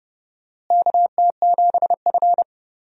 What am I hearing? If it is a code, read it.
Morse code, heard as a keyed tone, about 32 wpm, 700 Hz. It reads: KT7F
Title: USA Call Signs - 4 Characters - Repeated 2x - 30wpm